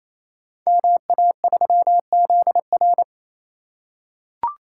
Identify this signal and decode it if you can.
Morse code, heard as a keyed tone, about 29 wpm, 700 Hz. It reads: MA3ZR
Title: Code Groups - 5 Letter:Numbers - 28wpm